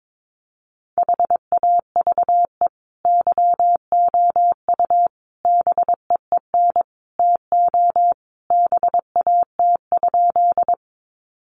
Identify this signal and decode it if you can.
Morse code, heard as a keyed tone, about 22 wpm, 700 Hz. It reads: HAVE YOU BEEN TO BAT?
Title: Sentences from Top 1500 Words 22wpm